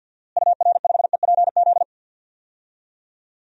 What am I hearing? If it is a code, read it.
Morse code, heard as a keyed tone, about 50 wpm, 700 Hz. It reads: UR5EPZ
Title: DXCC-100 Call Signs - 6 Characters - Repeated 2x - 50wpm